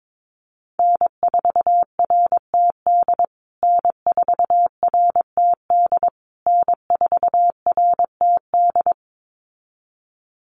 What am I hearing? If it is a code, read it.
Morse code, heard as a keyed tone, about 22 wpm, 700 Hz. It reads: N4RTD N4RTD N4RTD
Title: USA Call Signs Repeated 3x 22wpm